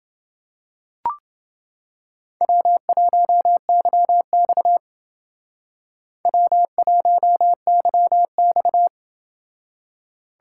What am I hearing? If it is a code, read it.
Morse code, heard as a keyed tone, about 29 wpm, 700 Hz. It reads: W1YX W1YX